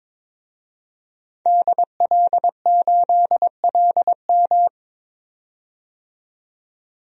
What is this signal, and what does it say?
Morse code, heard as a keyed tone, about 22 wpm, 700 Hz. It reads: DL8LM
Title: DXCC-100 Call Signs - 5 Characters - 22wpm